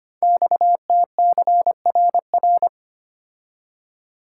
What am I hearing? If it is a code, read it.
Morse code, heard as a keyed tone, about 25 wpm, 700 Hz. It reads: XTCRR
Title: Code Groups - 5 Letters - 25wpm